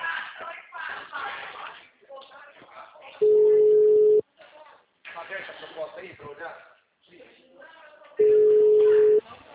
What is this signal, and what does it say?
Ringback tone (Europe, 425 Hz), with speech.
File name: out- -2007- - - .47574